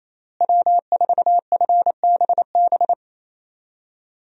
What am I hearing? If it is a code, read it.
Morse code, heard as a keyed tone, about 28 wpm, 700 Hz. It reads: W4FBB
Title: USA Call Signs - 5 Characters - Repeated 2x - 28wpm